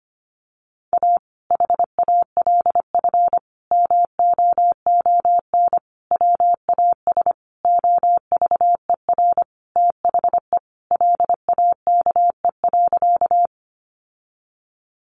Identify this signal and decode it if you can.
Morse code, heard as a keyed tone, about 25 wpm, 700 Hz. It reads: A HALF MOON WAS OVER THE LAKE.
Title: Sentences from Top 1100 Words 25wpm